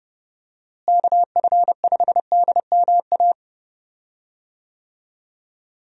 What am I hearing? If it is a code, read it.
Morse code, heard as a keyed tone, about 30 wpm, 700 Hz. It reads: KF5DMA
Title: USA Call Signs - 6 Characters - Repeated 3x - 30wpm